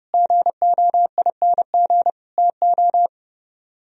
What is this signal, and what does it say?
Morse code, heard as a keyed tone, about 39 wpm, 700 Hz. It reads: GOING TO